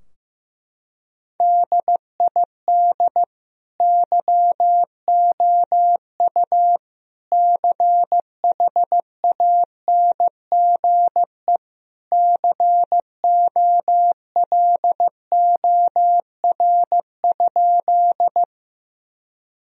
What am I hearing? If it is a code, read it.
Morse code, heard as a keyed tone, about 15 wpm, 700 Hz. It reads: DID YOU CHANGE COLOR?
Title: Sentences from Top 1000 Words - Review Part 2 - 15wpm